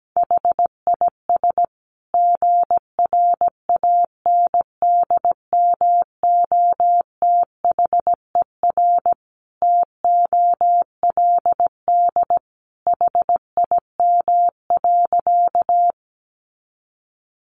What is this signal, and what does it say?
Morse code, heard as a keyed tone, about 17 wpm, 700 Hz. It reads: HIS GRANDMOTHER TOLD HIM.
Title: Sentences from Top 1200 Words 17wpm